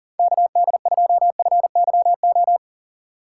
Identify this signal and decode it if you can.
Morse code, heard as a keyed tone, about 40 wpm, 700 Hz. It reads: KD2FYO
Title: USA Call Signs - 6 Characters - 3x Character Spacing - 40wpm